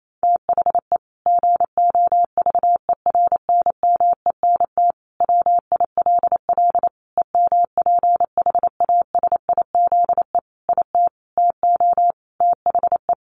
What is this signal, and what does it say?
Morse code, heard as a keyed tone, about 28 wpm, 700 Hz. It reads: THE GOVERNMENT WILL EMPHASIZE IT TO THE
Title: Sentences from Top 1500 Words 28wpm